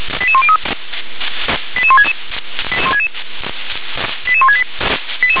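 Five-tone selective call (ZVEI), heard as quick runs of five tones, tones of about 70 ms. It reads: 88103 88166 88168
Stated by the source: ZVEI five-tone selective call received via SDR.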